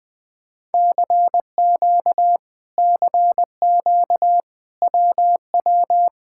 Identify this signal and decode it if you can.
Morse code, heard as a keyed tone, about 21 wpm, 700 Hz. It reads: CQ CQ WW